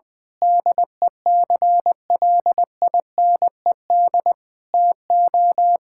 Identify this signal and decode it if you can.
Morse code, heard as a keyed tone, about 19 wpm, 700 Hz. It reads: DECLINED TO
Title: Sentences with 8-Word Phrases - Spoken Beginning - 20wpm